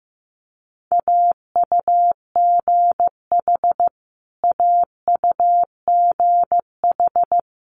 Morse code, heard as a keyed tone, about 15 wpm, 700 Hz. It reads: AUGH AUGH
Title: Multi-letter Phonemes Repeated 3x 15wpm